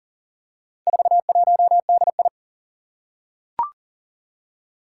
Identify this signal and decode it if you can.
Morse code, heard as a keyed tone, about 40 wpm, 700 Hz. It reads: 41DI
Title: Four Challenging Letter-Number Combinations 40wpm